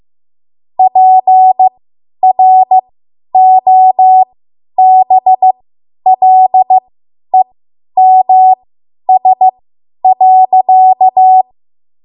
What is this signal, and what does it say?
Morse code, heard as a keyed tone, about 15 wpm, spaced as for 10 wpm, 750 Hz. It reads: PROBLEMS.
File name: Morse / 10WPM